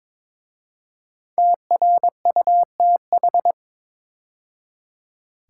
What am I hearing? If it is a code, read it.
Morse code, heard as a keyed tone, about 22 wpm, 700 Hz. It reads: TRUTH
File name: 5 Letter Words - First 3 Letters Given - 22wpm